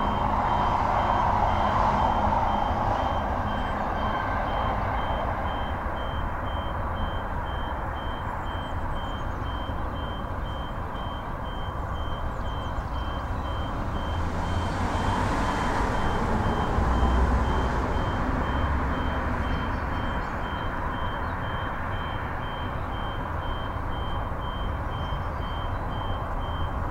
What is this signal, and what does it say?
There is no code.